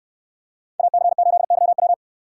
Morse code, heard as a keyed tone, about 66 wpm, 700 Hz. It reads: AC8YW